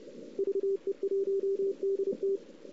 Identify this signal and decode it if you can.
Morse code, heard as a keyed tone, about 32 wpm, 400 Hz. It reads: VE1DT
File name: CW / 2008-ss-cw-k5tr-op-n5rz -02:00